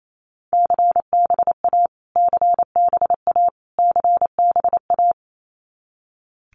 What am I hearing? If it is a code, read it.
Morse code, heard as a keyed tone, about 28 wpm, 700 Hz. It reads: CBA CBA CBA
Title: Abbreviations Repeated 3x - 28wpm